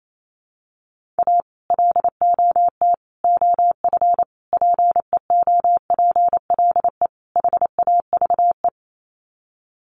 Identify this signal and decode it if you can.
Morse code, heard as a keyed tone, about 28 wpm, 700 Hz. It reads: A LOT OF PEOPLE HAVE
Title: Sets of 5 Words 28wpm